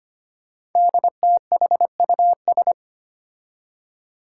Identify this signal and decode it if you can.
Morse code, heard as a keyed tone, about 25 wpm, 700 Hz. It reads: DTHUS
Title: Five Challenging Letter-Number Combinations 25wpm